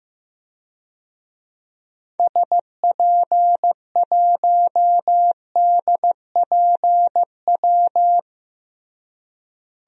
Morse code, heard as a keyed tone, about 15 wpm, 700 Hz. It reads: SP1DPW
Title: DXCC-100 Call Signs - 6 Characters - Repeated 2x - 15wpm